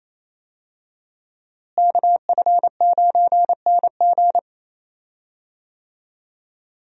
Morse code, heard as a keyed tone, about 28 wpm, 700 Hz. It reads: KF9NG
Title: USA Call Signs - 5 Characters - Repeated 3x - 28wpm